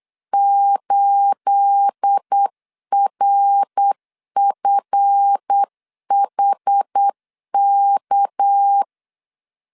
Morse code, heard as a keyed tone, about 8 wpm, 800 Hz. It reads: 8RFHK
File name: LEZIONI / 9 28 8wpm 3512